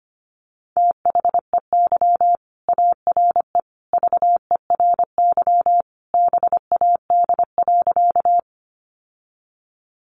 Morse code, heard as a keyed tone, about 25 wpm, 700 Hz. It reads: THEY ARE VERY BAD.
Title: Sentences from Top 700 Words 25wpm